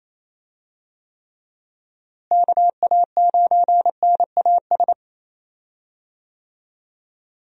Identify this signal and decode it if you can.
Morse code, heard as a keyed tone, about 28 wpm, 700 Hz. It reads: KA9NAS
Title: USA Call Signs - 6 Characters - Repeated 2x - 28wpm